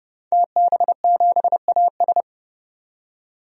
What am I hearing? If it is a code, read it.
Morse code, heard as a keyed tone, about 30 wpm, 700 Hz. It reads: TB7AS
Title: DXCC-100 Call Signs - 5 Characters - 30wpm